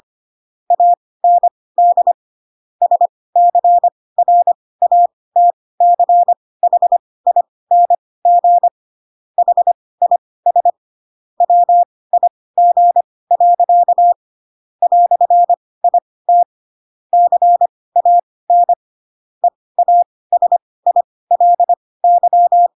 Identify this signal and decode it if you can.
Morse code, heard as a keyed tone, about 25 wpm, spaced as for 18 wpm, 700 Hz. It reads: AND SCRATCHING HIS WIG. "IT CAN EASILY
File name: Pinocchio / Pinocchio ch1 12 25 ik1hge